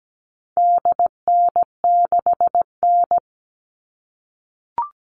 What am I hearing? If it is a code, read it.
Morse code, heard as a keyed tone, about 17 wpm, 700 Hz. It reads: DN6N
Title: All Cut Numbers - 4-Digit 17wpm